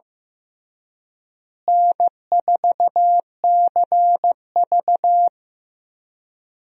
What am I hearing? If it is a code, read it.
Morse code, heard as a keyed tone, about 15 wpm, 700 Hz. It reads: N4CV
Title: USA Call Signs - 4 Characters - Repeated 3x - 15wpm